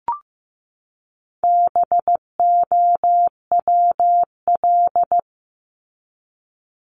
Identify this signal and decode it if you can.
Morse code, heard as a keyed tone, about 15 wpm, 700 Hz. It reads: BOWL
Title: Top 1001-1250 Words 15wpm